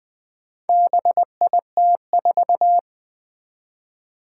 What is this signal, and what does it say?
Morse code, heard as a keyed tone, about 20 wpm, 700 Hz. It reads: BIT4
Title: Four Challenging Letter-Number Combinations 20wpm